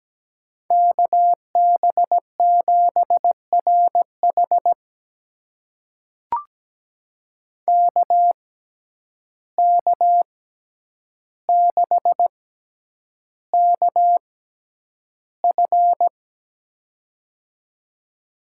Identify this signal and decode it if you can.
Morse code, heard as a keyed tone, about 17 wpm, spaced as for 9 wpm, 700 Hz. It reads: KB7RH KK6KF
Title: USA Call Signs - 6x Character Spacing - 17wpm